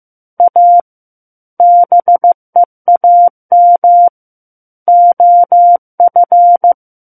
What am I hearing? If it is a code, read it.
Morse code, heard as a keyed tone, about 15 wpm, 700 Hz. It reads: A BEAM OF
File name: Chapter 6 - 15 WPM